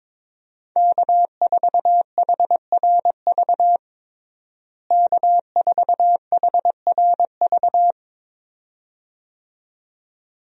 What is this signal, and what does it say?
Morse code, heard as a keyed tone, about 22 wpm, 700 Hz. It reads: K4HRV K4HRV
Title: USA Call Signs - 5 Characters - Repeated 2x - 22wpm